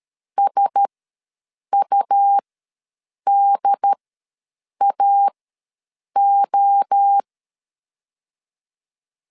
Morse code, heard as a keyed tone, about 13 wpm, spaced as for 7 wpm, 800 Hz. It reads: SUDAO
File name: LEZIONI / 5 LBVC / 5 LBV ws 12wpm 3015